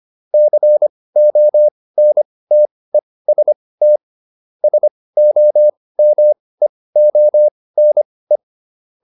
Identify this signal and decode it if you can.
Morse code, heard as a keyed tone, about 25 wpm, spaced as for 18 wpm, 600 Hz. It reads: CONTEST SOMEONE